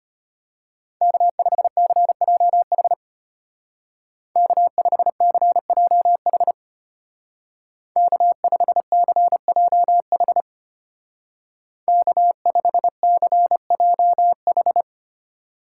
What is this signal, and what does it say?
Morse code, heard as a keyed tone, about 32 wpm, 700 Hz. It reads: K5CJH K5CJH K5CJH K5CJH